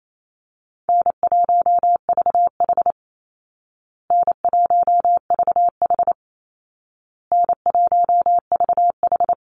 Morse code, heard as a keyed tone, about 28 wpm, 700 Hz. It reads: N1VH N1VH N1VH
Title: Call Signs Repeated 2x - 28wpm